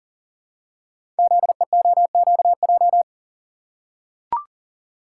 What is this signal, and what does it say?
Morse code, heard as a keyed tone, about 41 wpm, 700 Hz. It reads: ZEOQJ
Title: Code Groups - 5 Letters - 40wpm